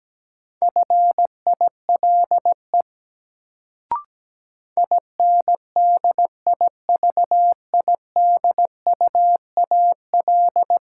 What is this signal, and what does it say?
Morse code, heard as a keyed tone, about 17 wpm, 700 Hz. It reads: FILE INDIVIDUAL
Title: Sentences from Top 1000 Words - Review Part 4 - New Word Drill - 17wpm